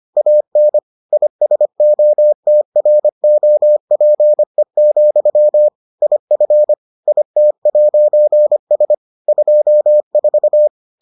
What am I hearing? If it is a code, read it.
Morse code, heard as a keyed tone, about 25 wpm, 600 Hz. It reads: AN ISOTROPE, IF IT'S 24